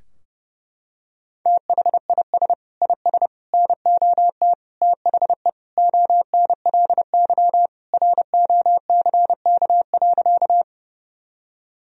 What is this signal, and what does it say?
Morse code, heard as a keyed tone, about 30 wpm, 700 Hz. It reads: THIS IS NOT THE ONLY ROCK.
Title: Sentences from Top 1000 Words - Review Part 2 - 30wpm